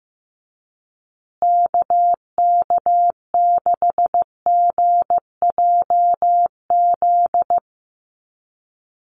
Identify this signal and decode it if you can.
Morse code, heard as a keyed tone, about 15 wpm, 700 Hz. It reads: KK6GJZ
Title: USA Call Signs Repeated 3x 15wpm